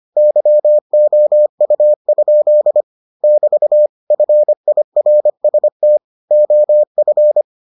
Morse code, heard as a keyed tone, about 25 wpm, 600 Hz. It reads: YOU? <BT> FIRST OF